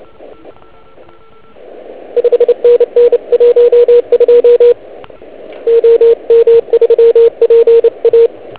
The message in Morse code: 5NN 12 OM3PA